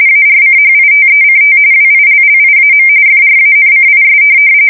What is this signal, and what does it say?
RTTY, about 45.45 baud. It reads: ARCHITECT WILLIAM FRANCIS
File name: fsk170 / USS-New-Jersey